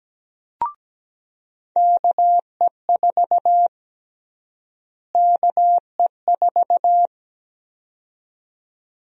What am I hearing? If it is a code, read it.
Morse code, heard as a keyed tone, about 17 wpm, 700 Hz. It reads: KE4 KE4
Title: USA Call Sign Prefixes - Repeated 2x - 17wpm